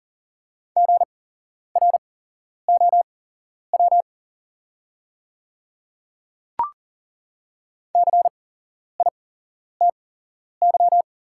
Morse code, heard as a keyed tone, about 40 wpm, spaced as for 11 wpm, 690 Hz. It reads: GROW CITY
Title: 8x Character Spacing - Top 300 Words - 40wpm